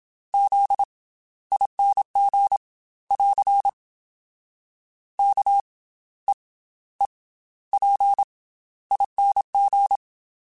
Morse code, heard as a keyed tone, about 27 wpm, 800 Hz. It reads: Z ING <AR> K E E P ING
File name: Special INGs / Special INGs-mp3s / Slow ing @25